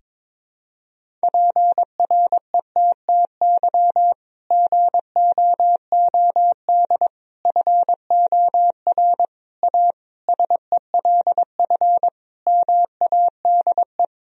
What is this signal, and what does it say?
Morse code, heard as a keyed tone, about 22 wpm, 700 Hz. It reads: PRETTY GOOD FOR A SELF MADE
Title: Sentences from Top 1000 Words - Review Part 1 - 22wpm